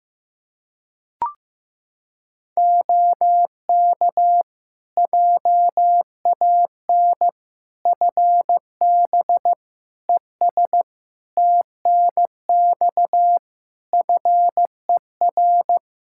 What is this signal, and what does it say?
Morse code, heard as a keyed tone, about 15 wpm, 700 Hz. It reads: OK JAN FB ES TNX FER
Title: K1USN SST QSO Practice 15wpm